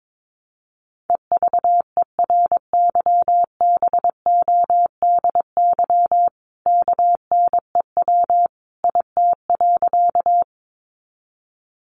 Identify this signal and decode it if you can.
Morse code, heard as a keyed tone, about 22 wpm, 700 Hz. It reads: EVERYBODY KNEW IT.
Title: Sentences from Top 1400 Words 22wpm